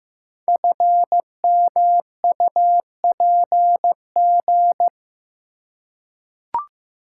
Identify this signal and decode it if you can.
Morse code, heard as a keyed tone, about 15 wpm, 700 Hz. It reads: FMUPG
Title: Code Groups - 5 Letters - 15wpm